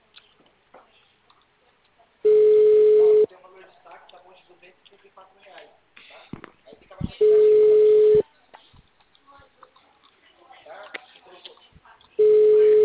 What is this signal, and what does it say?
Ringback tone (Europe, 425 Hz), with speech.